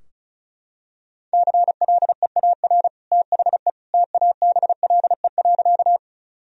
Morse code, heard as a keyed tone, about 36 wpm, 700 Hz. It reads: CLEAR THE TABLE.
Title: Sentences from Top 1000 Words - Review Part 2 - 35wpm